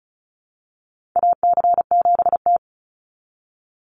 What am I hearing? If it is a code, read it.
Morse code, heard as a keyed tone, about 34 wpm, 700 Hz. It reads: AC7T
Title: Call Signs Repeated 2x - 35wpm